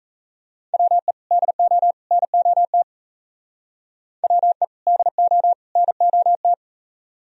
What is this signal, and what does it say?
Morse code, heard as a keyed tone, about 35 wpm, 700 Hz. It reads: WE DO NOT WE DO NOT